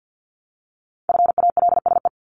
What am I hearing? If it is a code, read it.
Morse code, heard as a keyed tone, about 50 wpm, 700 Hz. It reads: FALSE
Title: Top 1501-1750 Words 50wpm